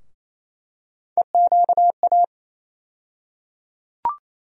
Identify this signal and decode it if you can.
Morse code, heard as a keyed tone, about 27 wpm, 700 Hz. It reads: EQA
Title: USA Call Sign Postfixes - 3 Letters 28wpm